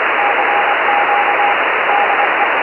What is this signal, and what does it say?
Morse code, heard as a keyed tone, about 24 wpm, 810 Hz, with radio noise. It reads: CQ DE
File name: JP1LRT-430CW